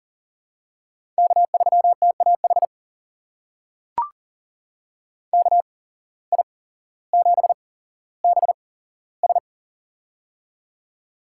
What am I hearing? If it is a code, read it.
Morse code, heard as a keyed tone, about 40 wpm, spaced as for 17 wpm, 700 Hz. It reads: K3TAH KI7BS
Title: USA Call Signs - 5 Characters - 8x Character Spacing - 40wpm